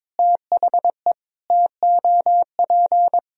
Morse code, heard as a keyed tone, about 22 wpm, 700 Hz. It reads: THE TOP
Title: Sentences from Top 1000 Words - Review Part 2 - 22wpm